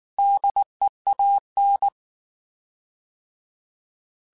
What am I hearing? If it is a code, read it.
Morse code, heard as a keyed tone, about 19 wpm, 800 Hz. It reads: DEAN
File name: Alphabetical / Alphabetical mp3s / Ds @18